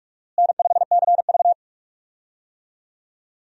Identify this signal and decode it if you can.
Morse code, heard as a keyed tone, about 46 wpm, 700 Hz. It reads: N5CV
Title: USA Call Signs - 4 Characters - Repeated 3x - 45wpm